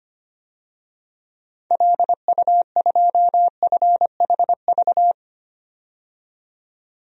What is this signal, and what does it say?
Morse code, heard as a keyed tone, about 25 wpm, 700 Hz. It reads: LU2FHV
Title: DXCC-100 Call Signs - 6 Characters - 25wpm